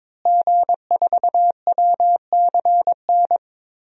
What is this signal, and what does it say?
Morse code, heard as a keyed tone, about 21 wpm, 700 Hz. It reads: G4WCN
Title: DXCC-100 Call Signs - 5 Characters - Repeated 2x - 22wpm